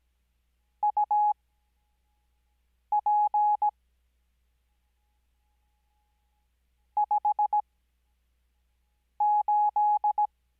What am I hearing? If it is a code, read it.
Morse code, heard as a keyed tone, about 17 wpm, spaced as for 5 wpm, 840 Hz. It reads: UP 58
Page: Farnsworth Spacing: 15WPM Character speed Spaced at 5WPM.